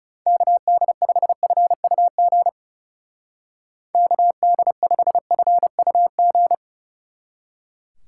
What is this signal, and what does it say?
Morse code, heard as a keyed tone, about 39 wpm, 700 Hz. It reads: KD5FUG KD5FUG